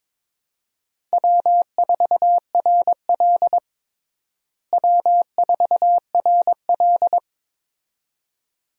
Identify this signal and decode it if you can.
Morse code, heard as a keyed tone, about 22 wpm, 700 Hz. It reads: W4RL W4RL
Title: USA Call Signs - 4 Characters - Repeated 2x - 22wpm